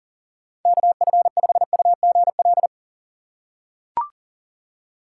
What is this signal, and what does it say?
Morse code, heard as a keyed tone, about 40 wpm, 700 Hz. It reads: KF5UGL
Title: USA Call Signs - 6 Characters - 2x Character Spacing - 40wpm